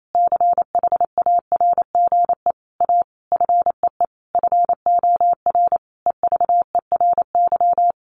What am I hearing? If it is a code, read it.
Morse code, heard as a keyed tone, about 28 wpm, 700 Hz. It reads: CHARGE A FEE FOR EVERY
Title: Sentences from Top 1500 Words 28wpm